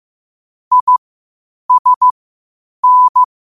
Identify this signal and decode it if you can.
Morse code, heard as a keyed tone, about 15 wpm, spaced as for 8 wpm, 1000 Hz. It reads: ISN
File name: qotdfiles / 8wpm